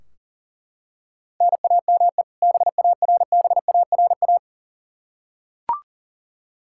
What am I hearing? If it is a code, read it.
Morse code, heard as a keyed tone, about 40 wpm, 700 Hz. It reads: NAME BARBARA
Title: Practice QSO - Exchange 1 - Beginning - 40wpm